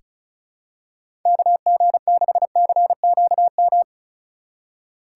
Morse code, heard as a keyed tone, about 36 wpm, 700 Hz. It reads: KG6CQM
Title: USA Call Signs - 6 Characters - 4x Character Spacing - 35wpm